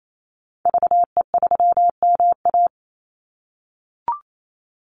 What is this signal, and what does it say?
Morse code, heard as a keyed tone, about 28 wpm, 700 Hz. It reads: VE3MA
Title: Call Signs - 28wpm